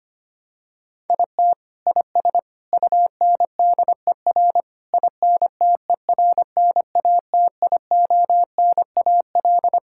Morse code, heard as a keyed tone, about 25 wpm, 700 Hz. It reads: IT IS UNDER INTERNATIONAL
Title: Sentences from Top 1000 Words - Review Part 6 - 25wpm